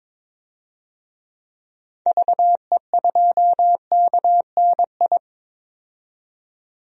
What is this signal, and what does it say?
Morse code, heard as a keyed tone, about 22 wpm, 700 Hz. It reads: VE2KNI
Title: DXCC-100 Call Signs - 6 Characters - 22wpm